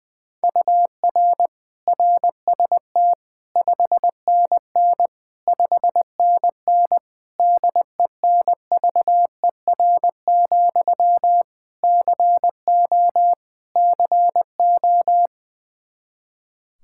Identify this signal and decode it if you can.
Morse code, heard as a keyed tone, about 20 wpm, 700 Hz. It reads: UR RST 5NN 5NN DENVER, CO CO
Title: Practice QSO - Exchange 1 - Beginning - 20wpm